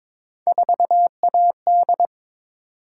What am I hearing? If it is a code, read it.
Morse code, heard as a keyed tone, about 22 wpm, 700 Hz. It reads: 4AD
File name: practice / 22wpm / Three Letter-Number Combinations - 22wpm